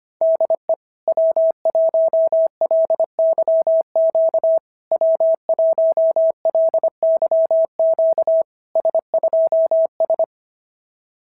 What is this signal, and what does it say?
Morse code, heard as a keyed tone, about 24 wpm, 650 Hz. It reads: DE W1LYQ W1LYQ S2S
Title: SOTA QSO Practice 25wpm